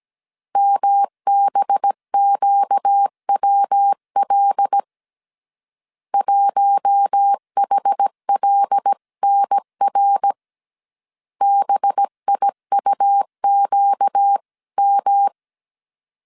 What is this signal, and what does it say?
Morse code, heard as a keyed tone, about 17 wpm, 800 Hz. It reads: MBQWL 1HLNR BIUQM
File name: 16wpm